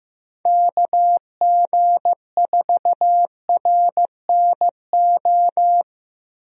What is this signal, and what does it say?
Morse code, heard as a keyed tone, about 14 wpm, 700 Hz. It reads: KG4RNO
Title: USA Call Signs - 6 Characters - Repeated 3x - 15wpm